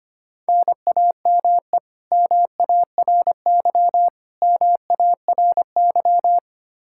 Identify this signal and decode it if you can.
Morse code, heard as a keyed tone, about 25 wpm, 700 Hz. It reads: NAME MARY MARY
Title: K1USN SST QSO Practice 25wpm